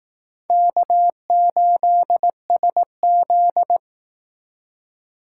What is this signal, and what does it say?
Morse code, heard as a keyed tone, about 18 wpm, 700 Hz. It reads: K8SZ
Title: Speed Racing - USA Call Signs 15wpm